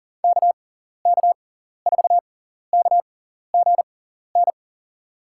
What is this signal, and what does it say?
Morse code, heard as a keyed tone, about 40 wpm, spaced as for 14 wpm, 700 Hz. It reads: KK4KGN
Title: USA Call Signs - 6 Characters - 6x Character Spacing - 40wpm